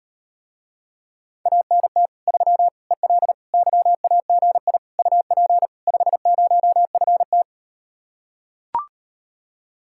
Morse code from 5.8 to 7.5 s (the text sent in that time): 50FT